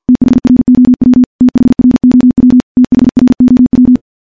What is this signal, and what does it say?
Morse code, heard as a keyed tone, about 51 wpm, 250 Hz. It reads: THROW THROW THROW
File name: morse / morse-training-50WPM